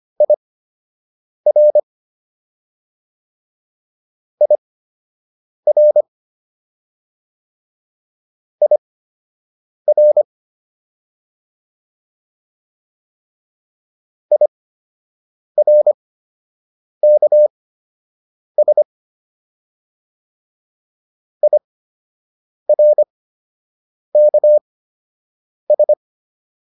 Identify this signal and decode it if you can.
Morse code, heard as a keyed tone, about 25 wpm, spaced as for 7 wpm, 600 Hz. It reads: IR IR IR IRKS IRKS